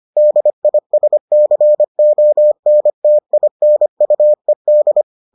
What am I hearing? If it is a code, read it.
Morse code, heard as a keyed tone, about 25 wpm, 600 Hz. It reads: DISCONTINUED